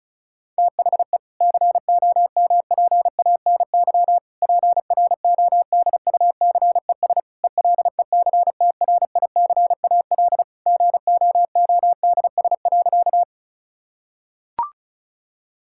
Morse code, heard as a keyed tone, about 36 wpm, 700 Hz. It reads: THE COMPANY PRODUCES ELECTRICAL GOODS.
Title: Sentences from Top 1000 Words - Review Part 6 - 35wpm